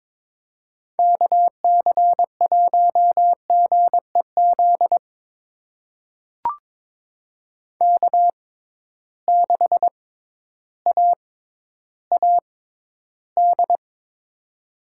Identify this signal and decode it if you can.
Morse code, heard as a keyed tone, about 22 wpm, spaced as for 12 wpm, 700 Hz. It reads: KC1GEZ K6AAD
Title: USA Call Signs - 6x Character Spacing - 22wpm